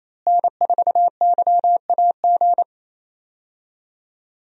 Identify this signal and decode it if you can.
Morse code, heard as a keyed tone, about 27 wpm, 700 Hz. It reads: N4YAG
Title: USA Call Signs - 5 Characters - Repeated 3x - 28wpm